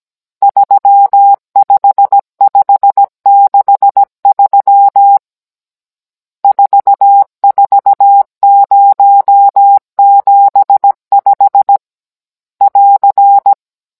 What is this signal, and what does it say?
Morse code, heard as a keyed tone, about 17 wpm, 800 Hz. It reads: 35563 44075 <AR>
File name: Cours-NR18-16mm